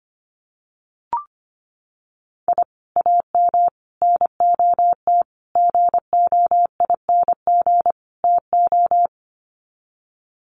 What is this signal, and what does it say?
Morse code, heard as a keyed tone, about 25 wpm, 700 Hz. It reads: I AM NOT GOING TO
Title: Sets of 5 Words Encore 25wpm